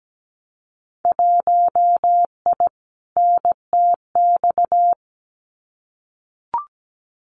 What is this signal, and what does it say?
Morse code, heard as a keyed tone, about 17 wpm, 700 Hz. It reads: 1I NTX
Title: Winter Field Day Exchanges 17wpm